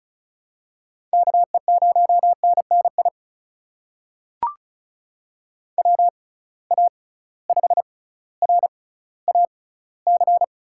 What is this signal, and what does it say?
Morse code, heard as a keyed tone, about 36 wpm, spaced as for 18 wpm, 700 Hz. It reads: KE0NNI WA5RAC
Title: USA Call Signs - 6 Characters - 6x Character Spacing - 35wpm